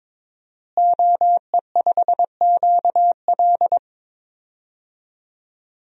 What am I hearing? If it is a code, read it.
Morse code, heard as a keyed tone, about 22 wpm, 700 Hz. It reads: OE5QL
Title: DXCC-100 Call Signs - 5 Characters - Repeated 2x - 22wpm